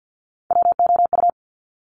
Morse code, heard as a keyed tone, about 50 wpm, 700 Hz. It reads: WOU
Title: Top Three Letter Combinations in Words 50wpm